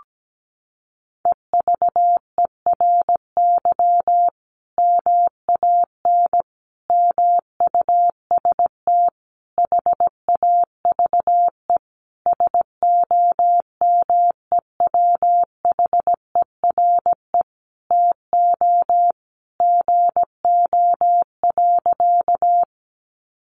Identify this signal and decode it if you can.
Morse code, heard as a keyed tone, about 17 wpm, 700 Hz. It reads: EVERY MAN MUST HAVE SOMEWHERE TO GO.
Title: Sentences from Top 1400 Words 17wpm